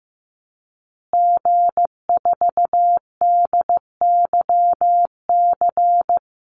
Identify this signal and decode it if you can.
Morse code, heard as a keyed tone, about 14 wpm, 700 Hz. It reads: G4DYC
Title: Call Signs Repeated 2x - 15wpm